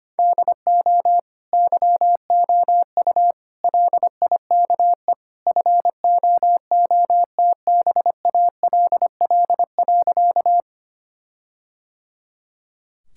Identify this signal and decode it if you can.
Morse code, heard as a keyed tone, about 25 wpm, 700 Hz. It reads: DO YOU LIKE FOOTBALL.
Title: Sentences from Top 1600 Words 25wpm